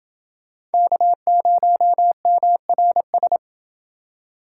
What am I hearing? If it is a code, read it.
Morse code, heard as a keyed tone, about 29 wpm, 700 Hz. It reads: K0MRS